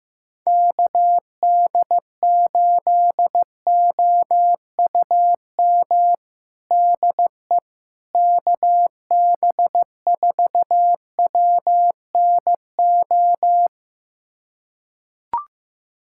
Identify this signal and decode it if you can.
Morse code, heard as a keyed tone, about 15 wpm, 700 Hz. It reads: KD8OUM DE KB4WNO
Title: USA Callsign DE Callsign 15wpm